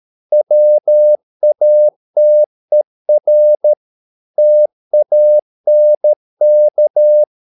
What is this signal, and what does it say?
Morse code, heard as a keyed tone, about 13 wpm, 600 Hz. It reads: WATER TANK